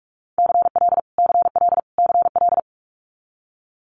Morse code, heard as a keyed tone, about 46 wpm, 700 Hz. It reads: CL CL CL
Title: Multi-letter Phonemes Repeated 3x 45wpm